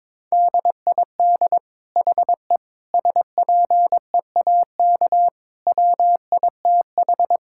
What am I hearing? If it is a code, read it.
Morse code, heard as a keyed tone, about 22 wpm, 700 Hz. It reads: DID HE SPEAK WITH
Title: Sentences from Top 1000 Words - Review Part 6 - 22wpm